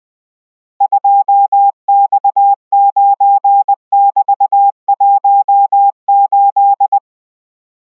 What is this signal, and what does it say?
Morse code, heard as a keyed tone, about 20 wpm, 800 Hz. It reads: 2X9<BT>18